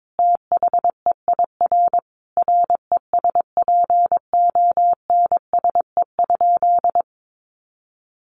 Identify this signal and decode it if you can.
Morse code, heard as a keyed tone, about 22 wpm, 700 Hz. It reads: THEIR RESPONSE?
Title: Sentences from Top 1400 Words 22wpm